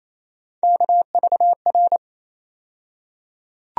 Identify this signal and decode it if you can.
Morse code, heard as a keyed tone, about 28 wpm, 700 Hz. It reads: KVR
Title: USA Call Sign Postfixes - 3 Letters 28wpm